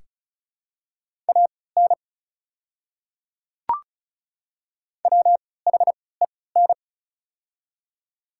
Morse code, heard as a keyed tone, about 36 wpm, spaced as for 20 wpm, 700 Hz. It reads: AN WHEN
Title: 3x Character Spacing - Top 300 Words - 35wpm